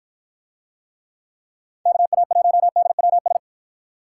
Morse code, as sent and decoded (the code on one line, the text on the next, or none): -.- .- .---- -.. .-- ...
KA1DWS